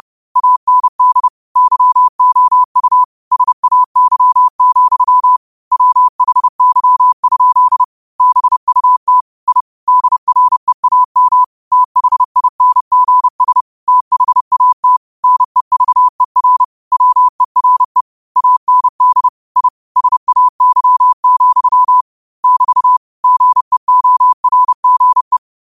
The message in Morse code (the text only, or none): AND YOU SAY, WHY? BUT I DREAM THINGS THAT NEVER WERE AND I SAY, <BT> GEORGE